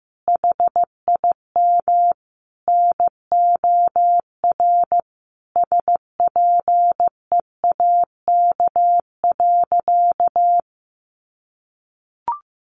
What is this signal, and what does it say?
Morse code, heard as a keyed tone, about 15 wpm, 700 Hz. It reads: HIM NOR SPEAK.
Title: Sentences from Top 1500 Words 15wpm